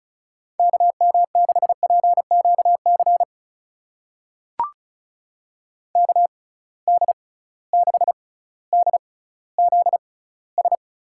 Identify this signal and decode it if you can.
Morse code, heard as a keyed tone, about 34 wpm, spaced as for 18 wpm, 700 Hz. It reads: KM6PQC KD6DZS
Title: USA Call Signs - 6 Characters - 6x Character Spacing - 35wpm